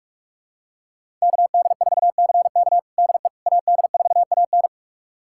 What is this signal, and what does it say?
Morse code, heard as a keyed tone, about 46 wpm, 700 Hz. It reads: KD4CK DE AD4AN
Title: USA Callsign DE Callsign - 5 Characters - 45wpm